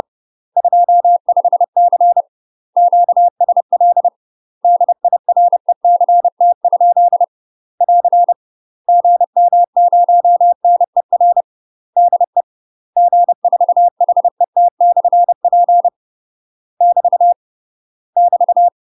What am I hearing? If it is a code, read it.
Morse code, heard as a keyed tone, about 30 wpm, 700 Hz. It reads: 25C QSL DIRECT? <AR> GM0NER DE G4HET/P <BT> <BT>
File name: UK MASTER QSOs TRAINING / 30WPM / UK MASTER QSOs 0017